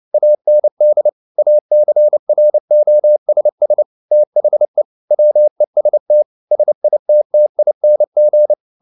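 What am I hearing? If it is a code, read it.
Morse code, heard as a keyed tone, about 29 wpm, 600 Hz. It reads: AND ACROSS THE WEST SITTING
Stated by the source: Morse code at 29 wpm, sending Ken Follett's novel Never in English, from CW med Gnister.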